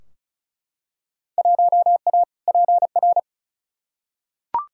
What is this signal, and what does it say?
Morse code, heard as a keyed tone, about 34 wpm, 700 Hz. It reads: 1A PR
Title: Field Day Exchanges - 35wpm